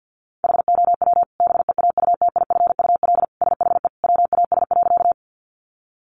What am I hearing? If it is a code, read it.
Morse code, heard as a keyed tone, about 50 wpm, 700 Hz. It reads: HOW BEAUTIFUL SHE WAS.
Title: Sentences from Top 900 Words 50wpm